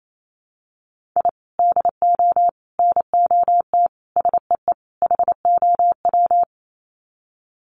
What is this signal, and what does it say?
Morse code, heard as a keyed tone, about 28 wpm, 700 Hz. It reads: I DO NOT SEE HOW
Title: Sets of 5 Words 28wpm